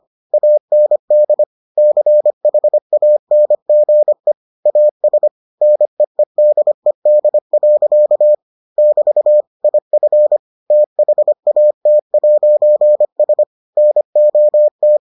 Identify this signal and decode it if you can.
Morse code, heard as a keyed tone, about 25 wpm, 600 Hz. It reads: AND CHANGE AS NEEDED. <BT> IF THAT'S NOT